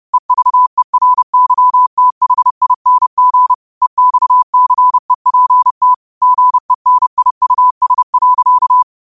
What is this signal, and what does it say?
Morse code, heard as a keyed tone, about 30 wpm, 1000 Hz. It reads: EVERYTHING EXCEPT GENIUS.
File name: qotdfiles / 30wpm